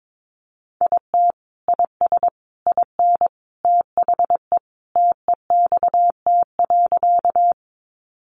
Morse code, heard as a keyed tone, about 22 wpm, 700 Hz. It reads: IT IS IN THE TEXT.
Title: Sentences from Top 1200 Words 22wpm